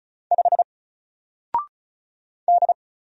Morse code, heard as a keyed tone, about 36 wpm, 700 Hz. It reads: 5 D
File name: Single-Letter-Number-35wpm